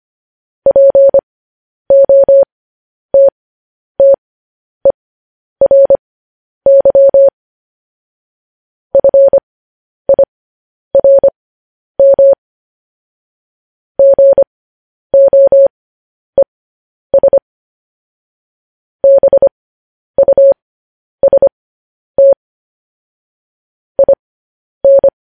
Morse code, heard as a keyed tone, about 25 wpm, spaced as for 10 wpm, 550 Hz. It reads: POTTERY FIRM GOES BUST IN
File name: podcast / News headlines-1x-25x10